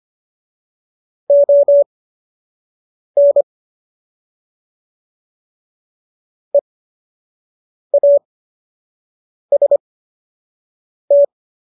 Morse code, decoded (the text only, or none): ON EAST